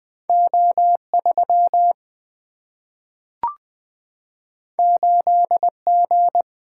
Morse code, heard as a keyed tone, about 20 wpm, 700 Hz. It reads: O 3 8 G
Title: Single-Letter Single-Number 20wpm